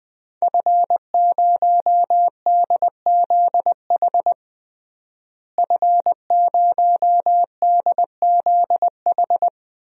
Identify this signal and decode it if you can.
Morse code, heard as a keyed tone, about 20 wpm, 700 Hz. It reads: F0DZH F0DZH
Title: DXCC-100 Call Signs - 5 Characters - Repeated 2x - 20wpm